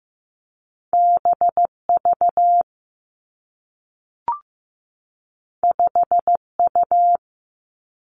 Morse code, heard as a keyed tone, about 15 wpm, 700 Hz. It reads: B V 5 U
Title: Two Challenging Letter-Number Combinations 15wpm